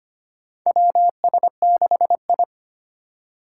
Morse code, heard as a keyed tone, about 25 wpm, 700 Hz. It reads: WS6I
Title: USA Call Signs - 4 Characters - 4x Character Spacing - 25wpm